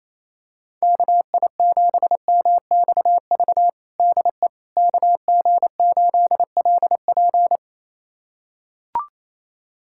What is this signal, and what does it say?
Morse code, heard as a keyed tone, about 27 wpm, 700 Hz. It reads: KI7MXV DE KG8LP
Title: USA Callsign DE Callsign 28wpm